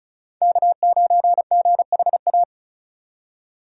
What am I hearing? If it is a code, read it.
Morse code, heard as a keyed tone, about 36 wpm, 700 Hz. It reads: K9GHA
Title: USA Call Signs - 5 Characters - 6x Character Spacing - 35wpm